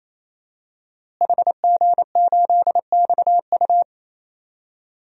Morse code, heard as a keyed tone, about 28 wpm, 700 Hz. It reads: HG8XU
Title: DXCC-100 Call Signs - 5 Characters - Repeated 3x - 28wpm